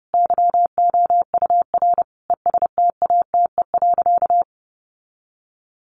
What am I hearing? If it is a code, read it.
Morse code, heard as a keyed tone, about 30 wpm, 700 Hz. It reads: YOUR ESTATE.
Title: Sentences from Top 1200 Words 30wpm